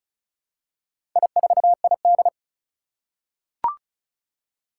Morse code, heard as a keyed tone, about 34 wpm, 700 Hz. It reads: I4ID
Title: Four Challenging Letter-Number Combinations 35wpm